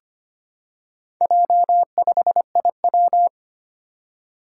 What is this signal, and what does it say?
Morse code, heard as a keyed tone, about 25 wpm, 700 Hz. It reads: J5IW
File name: practice / 25wpm / Four Letter-Number Combinations - 25wpm